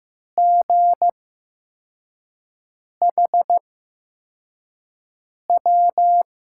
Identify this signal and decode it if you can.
Morse code, heard as a keyed tone, about 15 wpm, spaced as for 4 wpm, 700 Hz. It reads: GHW
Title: USA Call Signs - 6 Characters - 8x Character Spacing - 15wpm